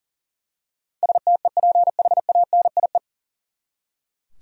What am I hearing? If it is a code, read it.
Morse code, heard as a keyed tone, about 40 wpm, 700 Hz. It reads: STEPHANIE
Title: USA Common Names - 40wpm